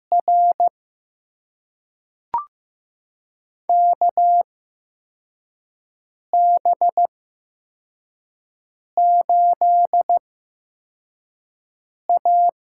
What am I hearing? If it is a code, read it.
Morse code, heard as a keyed tone, about 15 wpm, spaced as for 4 wpm, 700 Hz. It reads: R KB8A
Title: USA Call Signs - 6 Characters - 8x Character Spacing - 15wpm